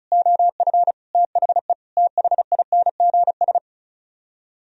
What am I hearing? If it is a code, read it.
Morse code, heard as a keyed tone, about 43 wpm, 700 Hz. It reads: OF THE THINGS
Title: Speed-Racing - Sets of 5 Words 35wpm